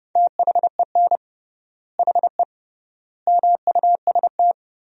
Morse code, heard as a keyed tone, about 30 wpm, 700 Hz. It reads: THEN HE MUST
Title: 3x Word-Spacing - Sentences from Top 500 Words - 30wpm